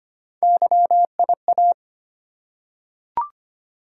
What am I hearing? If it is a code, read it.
Morse code, heard as a keyed tone, about 25 wpm, 700 Hz. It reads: YIA
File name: Three Letter-Number Combinations - 25wpm